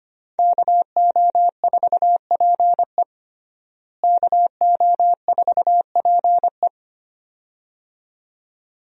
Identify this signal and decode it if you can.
Morse code, heard as a keyed tone, about 25 wpm, 700 Hz. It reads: KO4PE KO4PE
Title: USA Call Signs - 5 Characters - Repeated 2x - 25wpm